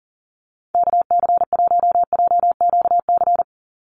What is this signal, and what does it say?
Morse code, heard as a keyed tone, about 40 wpm, 700 Hz. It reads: KC1JQC
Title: USA Call Signs 40wpm